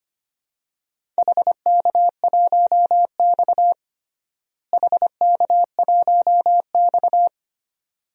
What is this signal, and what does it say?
Morse code, heard as a keyed tone, about 25 wpm, 700 Hz. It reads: HK1X HK1X
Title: DXCC-100 Call Signs - 4 Characters - Repeated 2x - 25wpm